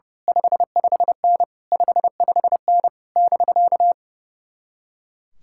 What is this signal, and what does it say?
Morse code, heard as a keyed tone, about 30 wpm, 700 Hz. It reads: 55N 55N <BK>
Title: Speed-Racing Common QSO Phrases Encore - 30wpm